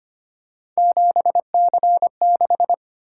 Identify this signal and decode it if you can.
Morse code, heard as a keyed tone, about 25 wpm, 700 Hz. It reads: 7C6
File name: Three Letter-Number Combinations - 25wpm